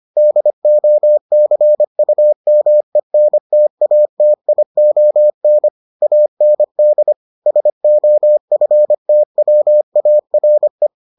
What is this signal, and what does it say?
Morse code, heard as a keyed tone, about 25 wpm, 600 Hz. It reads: DOCUMENTATION AND SOFTWARE